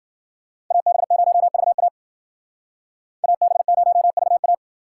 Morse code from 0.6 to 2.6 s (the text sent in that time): AB0VA